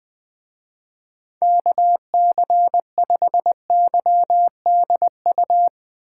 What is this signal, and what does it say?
Morse code, heard as a keyed tone, about 20 wpm, 700 Hz. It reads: KC5YDU
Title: Speed Racing - USA 2x3 Call Signs 20wpm